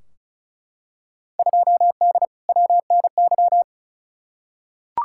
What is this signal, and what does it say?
Morse code, heard as a keyed tone, about 34 wpm, 700 Hz. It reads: 2D WNY
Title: Field Day Exchanges - 35wpm